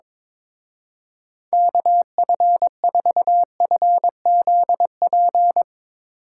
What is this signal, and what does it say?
Morse code, heard as a keyed tone, about 22 wpm, 700 Hz. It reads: KF4FZP
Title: Speed Racing - USA 2x3 Call Signs 22wpm